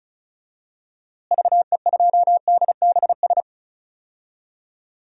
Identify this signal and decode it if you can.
Morse code, heard as a keyed tone, about 34 wpm, 700 Hz. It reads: VE2DBS
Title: DXCC-100 Call Signs - 6 Characters - Repeated 2x - 35wpm